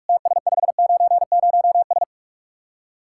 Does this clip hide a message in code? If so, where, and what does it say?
Yes: Morse code from 0.1 to 2.0 s, heard as a keyed tone, about 44 wpm, 700 Hz. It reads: TS590S